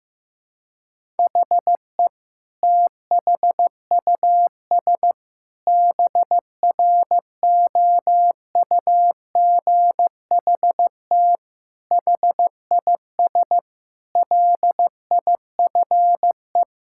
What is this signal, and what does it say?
Morse code, heard as a keyed tone, about 15 wpm, 700 Hz. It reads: HE THUS BROUGHT HIS LIFE
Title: Sentences from Top 1000 Words - Review Part 5 - 15wpm